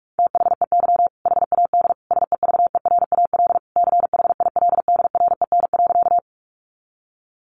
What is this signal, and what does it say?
Morse code, heard as a keyed tone, about 44 wpm, 700 Hz. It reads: THEY HAD SEVERAL CHILDREN.
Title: Sentences from Top 400 Words 45wpm